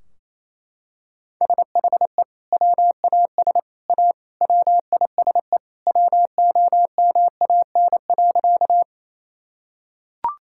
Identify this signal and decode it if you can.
Morse code, heard as a keyed tone, about 28 wpm, 700 Hz. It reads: SHE WAS A WISE WOMAN.
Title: Sentences from Top 1600 Words 28wpm